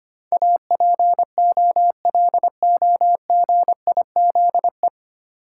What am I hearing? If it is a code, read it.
Morse code, heard as a keyed tone, about 25 wpm, 690 Hz. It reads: APOLOGIZE
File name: practice / 25wpm / 9 Letter Words - First 3 Letters Given - 25wpm